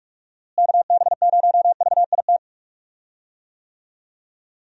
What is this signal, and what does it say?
Morse code, heard as a keyed tone, about 45 wpm, 700 Hz. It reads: KB0VIT
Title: USA Call Signs - 6 Characters - Repeated 3x - 45wpm